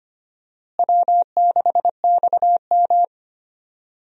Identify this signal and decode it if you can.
Morse code, heard as a keyed tone, about 25 wpm, 700 Hz. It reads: W6XM
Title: USA Call Signs - 4 Characters - Repeated 2x - 25wpm